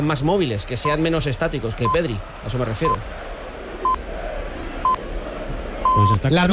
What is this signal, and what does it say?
Radio time signal (time pips), with speech.